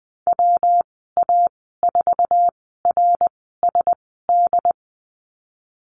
Morse code, heard as a keyed tone, about 20 wpm, spaced as for 15 wpm, 700 Hz. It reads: WA4RSD
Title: 186 Lesson 30g - Letter b - Call Signs 20wpm